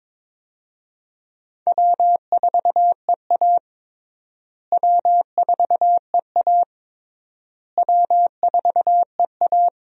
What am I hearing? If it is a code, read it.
Morse code, heard as a keyed tone, about 22 wpm, 700 Hz. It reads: W4EA W4EA W4EA
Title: USA Call Signs - 4 Characters - Repeated 3x - 22wpm